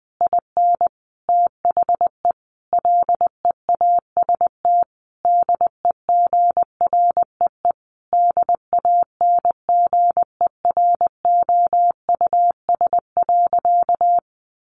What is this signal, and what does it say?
Morse code, heard as a keyed tone, about 20 wpm, 700 Hz. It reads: IN THE LEAST DEGREE DANGEROUS.
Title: Sentences from Top 1500 Words 20wpm